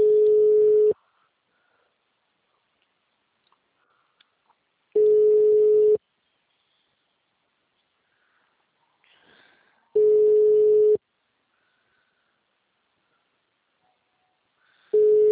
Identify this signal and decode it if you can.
Ringback tone (Europe, 425 Hz).
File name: out- -2001- - - .72542